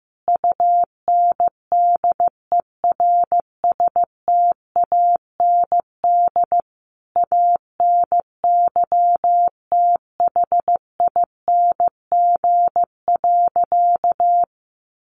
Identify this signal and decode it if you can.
Morse code, heard as a keyed tone, about 15 wpm, 700 Hz. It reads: UNDERSTAND ANYTHING.
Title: Sentences from Top 1000 Words 15wpm